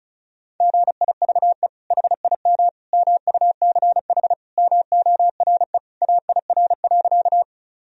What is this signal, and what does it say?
Morse code, heard as a keyed tone, about 36 wpm, 700 Hz. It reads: GIVE HIM MUCH MORE AIR.
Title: Sentences from Top 1000 Words - Review Part 1 - 35wpm